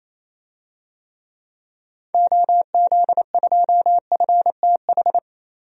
Morse code, heard as a keyed tone, about 28 wpm, 700 Hz. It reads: OZ2FTH
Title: DXCC-100 Call Signs - 6 Characters - 28wpm